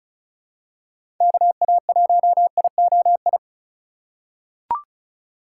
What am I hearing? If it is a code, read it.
Morse code, heard as a keyed tone, about 34 wpm, 700 Hz. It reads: KA1IOI
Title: USA Call Signs - 6 Characters - 3x Character Spacing - 35wpm